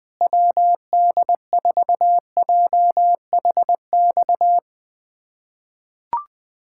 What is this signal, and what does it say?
Morse code, heard as a keyed tone, about 21 wpm, 700 Hz. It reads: WD4JHX
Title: USA Callsign DE Callsign - 6 Characters - 20wpm